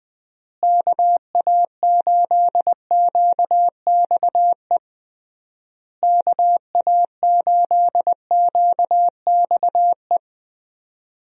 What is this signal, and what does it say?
Morse code, heard as a keyed tone, about 20 wpm, 700 Hz. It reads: KA8QXE KA8QXE
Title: USA Call Signs - 6 Characters - Repeated 3x - 20wpm